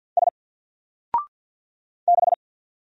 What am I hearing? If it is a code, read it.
Morse code, heard as a keyed tone, about 48 wpm, 700 Hz. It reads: S 6
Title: Single Character (Letter:Number:Punctuation:Prosign) Alt 50wpm